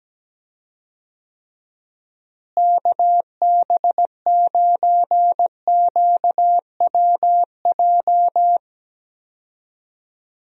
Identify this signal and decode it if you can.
Morse code, heard as a keyed tone, about 17 wpm, 700 Hz. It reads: KB9QWJ
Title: USA Call Signs - 6 Characters - Repeated 2x - 17wpm